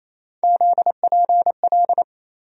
Morse code, heard as a keyed tone, about 28 wpm, 700 Hz. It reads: ZPL
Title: USA Call Sign Postfixes - 3 Letters 28wpm